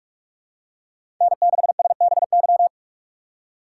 Morse code, heard as a keyed tone, about 44 wpm, 700 Hz. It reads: N6SBY
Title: USA Call Signs - 2x Character Spacing - 45wpm